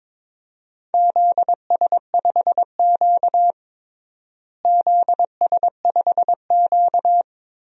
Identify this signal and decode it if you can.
Morse code, heard as a keyed tone, about 22 wpm, 700 Hz. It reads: ZS5Q ZS5Q
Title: DXCC-100 Call Signs - 4 Characters - Repeated 2x - 22wpm